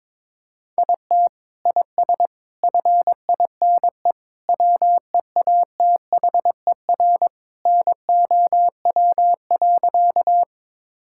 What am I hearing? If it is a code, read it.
Morse code, heard as a keyed tone, about 21 wpm, 700 Hz. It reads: IT IS FINE WEATHER NOW.
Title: Sentences from Top 1000 Words - Review Part 5 - 22wpm